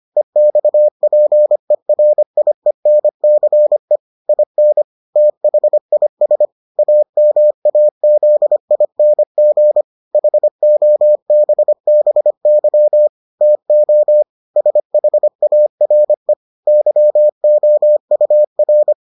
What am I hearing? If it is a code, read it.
Morse code, heard as a keyed tone, about 25 wpm, 600 Hz. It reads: EXPERIENCE IN THIS AMAZING HOBBY TO SHARE YOUR